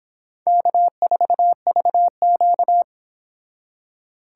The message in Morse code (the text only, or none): K4VQ